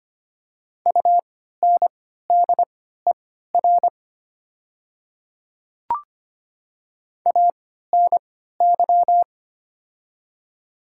Morse code, heard as a keyed tone, about 25 wpm, spaced as for 14 wpm, 700 Hz. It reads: UNDER ANY
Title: 3x Character Spacing - Top 200 Words - 25wpm